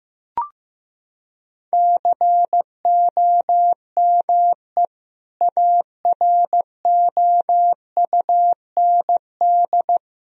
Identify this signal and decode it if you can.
Morse code, heard as a keyed tone, about 15 wpm, 700 Hz. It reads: COME AROUND
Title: Sentences from Top 1000 Words - Review Part 3 - 15wpm